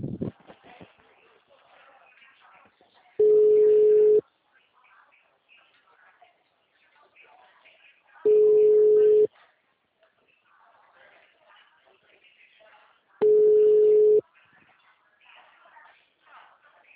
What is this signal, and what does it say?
Ringback tone (Europe, 425 Hz), with speech.
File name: out- -2012- - - .4